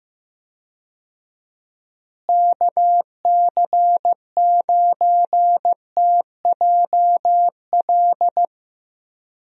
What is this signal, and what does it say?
Morse code, heard as a keyed tone, about 15 wpm, 700 Hz. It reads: KC9TJL
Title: USA Call Signs - 6 Characters - Repeated 3x - 15wpm